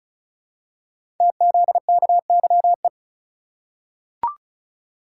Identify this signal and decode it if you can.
Morse code, heard as a keyed tone, about 36 wpm, 690 Hz. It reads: TZKYE
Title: Code Groups - 5 Letters - 35wpm